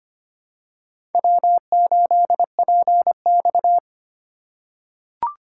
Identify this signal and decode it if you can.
Morse code, heard as a keyed tone, about 25 wpm, 700 Hz. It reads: W8PX
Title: USA Call Signs - 4 Characters - 2x Character Spacing - 25wpm